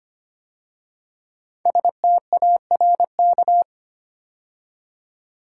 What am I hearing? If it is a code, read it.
Morse code, heard as a keyed tone, about 25 wpm, 700 Hz. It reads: STARK
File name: 5 Letter Words - First 3 Letters Given - 25wpm